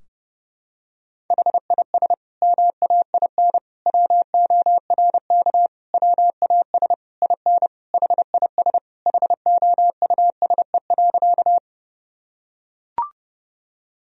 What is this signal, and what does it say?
Morse code, heard as a keyed tone, about 30 wpm, 700 Hz. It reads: HIS MAIN WORK WAS IN HIS HOUSE.
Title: Sentences from Top 1000 Words - Review Part 2 - 30wpm